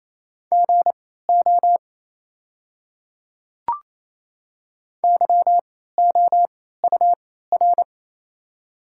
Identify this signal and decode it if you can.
Morse code, heard as a keyed tone, about 28 wpm, spaced as for 16 wpm, 700 Hz. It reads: GO YOUR
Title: 3x Character Spacing - Top 100 Words - 28wpm